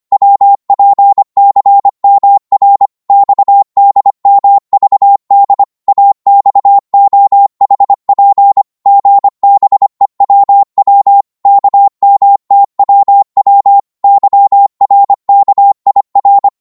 Morse code, heard as a keyed tone, about 24 wpm, 800 Hz. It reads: WPCMR XDMVD AXOHP GBEWW KMTWW YRKIR
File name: Morse / ltr-25wpm